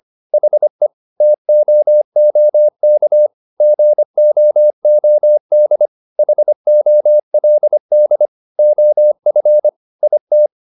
Morse code, heard as a keyed tone, about 25 wpm, 600 Hz. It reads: HE TOOK GOOD HOLD OF IT